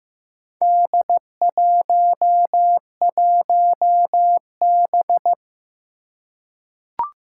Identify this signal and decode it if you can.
Morse code, heard as a keyed tone, about 15 wpm, 700 Hz. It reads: D11B
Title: Four Challenging Letter-Number Combinations 15wpm